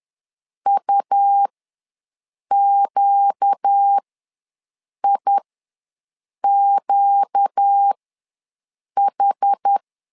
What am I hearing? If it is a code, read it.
Morse code, heard as a keyed tone, about 10 wpm, spaced as for 6 wpm, 800 Hz. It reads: UQIQH
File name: LEZIONI / 4 Q ws 10wpm 2306